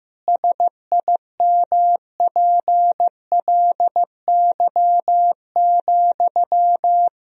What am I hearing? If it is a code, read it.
Morse code, heard as a keyed tone, about 15 wpm, 700 Hz. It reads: SIMPLY,
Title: Sentences from Top 1000 Words - Review Part 4 - 15wpm